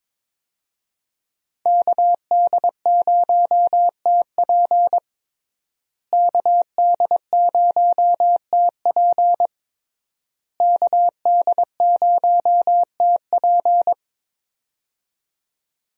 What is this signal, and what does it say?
Morse code, heard as a keyed tone, about 22 wpm, 700 Hz. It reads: KD0TP KD0TP KD0TP
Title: USA Call Signs - 5 Characters - Repeated 3x - 22wpm